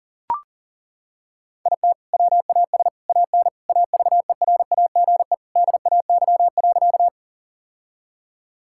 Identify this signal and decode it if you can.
Morse code, heard as a keyed tone, about 40 wpm, 700 Hz. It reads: IT WAS AN AVERAGE DAY.
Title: Sentences from Top 1000 Words - Review Part 6 - 40wpm